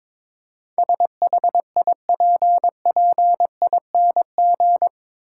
Morse code, heard as a keyed tone, about 22 wpm, 700 Hz. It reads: SHIPPING
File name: practice / 22wpm / Eight Letter Words - 22wpm